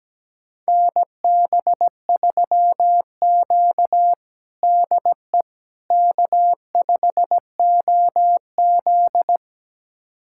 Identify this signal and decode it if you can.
Morse code, heard as a keyed tone, about 17 wpm, 700 Hz. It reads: NB3Q DE K5OZ
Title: USA Callsign DE Callsign - 4 Characters - 17wpm